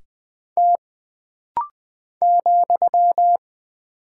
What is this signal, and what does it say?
Morse code, heard as a keyed tone, about 20 wpm, 700 Hz. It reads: T ,
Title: Single Character (Letter:Number:Punctuation:Prosign) - Rapid-Fire - 20wpm